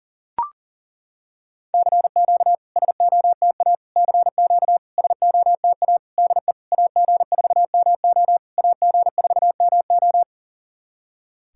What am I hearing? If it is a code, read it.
Morse code, heard as a keyed tone, about 40 wpm, 700 Hz. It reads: CQ SOTA CQ SOTA DE AG4MO AG4MO
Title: SOTA QSO Practice 40wpm